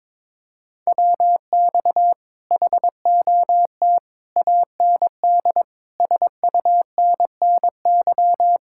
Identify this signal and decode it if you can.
Morse code, heard as a keyed tone, about 22 wpm, 700 Hz. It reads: WX HOT AND SUNNY
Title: Speed-Racing Common QSO Phrases Encore - 22wpm